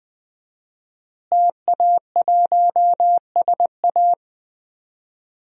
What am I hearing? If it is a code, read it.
Morse code, heard as a keyed tone, about 20 wpm, 700 Hz. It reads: TA1SA
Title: DXCC-100 Call Signs - 5 Characters - Repeated 2x - 20wpm